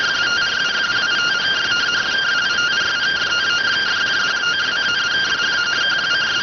Two-tone teleprinter signal, about 170 Hz shift.